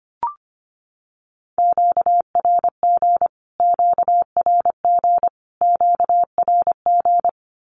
Morse code, heard as a keyed tone, about 25 wpm, 700 Hz. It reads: QRG QRG QRG
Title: Q-Codes Repeated 3x 25wpm